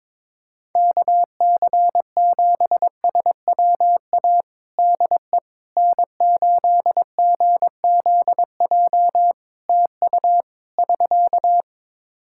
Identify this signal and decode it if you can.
Morse code, heard as a keyed tone, about 21 wpm, 700 Hz. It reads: KC7SWA DE N8GZJ TU <SK>
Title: Practice QSO - Exchange 4 - Ending - 22wpm